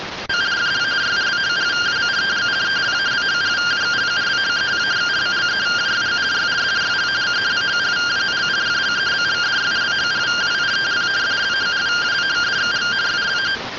Two-tone teleprinter signal, about 170 Hz shift.
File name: findings / CT 1 Scout 0001.gen-13 rtty rx